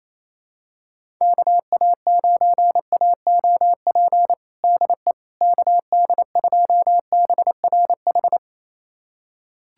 Morse code, heard as a keyed tone, about 27 wpm, 700 Hz. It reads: KA9AOP DE KD2BRH
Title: USA Callsign DE Callsign - 6 Characters - 28wpm